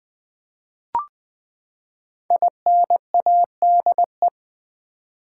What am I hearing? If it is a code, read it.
Morse code, heard as a keyed tone, about 20 wpm, 700 Hz. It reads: INADE
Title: Build A Word - 5 Letters At A Time - 10-Letter Words - 20wpm